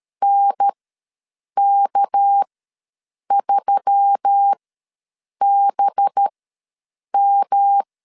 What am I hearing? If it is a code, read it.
Morse code, heard as a keyed tone, about 13 wpm, spaced as for 7 wpm, 800 Hz. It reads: NK3BM
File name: LEZIONI / 7 GF9Z / 7 GF ws 12wpm 3140